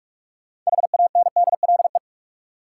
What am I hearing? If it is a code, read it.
Morse code, heard as a keyed tone, about 46 wpm, 700 Hz. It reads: HANDLE
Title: Sentences from Top 1100 Words - New Word Drill Alt - 45wpm